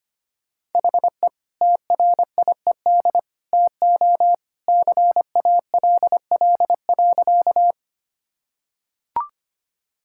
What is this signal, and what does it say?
Morse code, heard as a keyed tone, about 25 wpm, 700 Hz. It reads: HE TRIED TO CALL.
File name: practice / 25wpm / Sentences with 2-Word Phrases - 25wpm